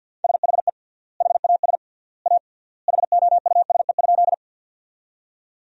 Morse code, heard as a keyed tone, about 49 wpm, 700 Hz. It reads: SHE HAS A HOUSE?
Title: 3x Word-Spacing - Sentences from Top 200 Words - 50wpm